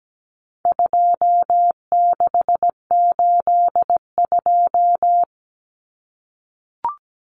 Morse code, heard as a keyed tone, about 17 wpm, 700 Hz. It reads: 2682
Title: Common Cut Numbers - 4-Digit 17wpm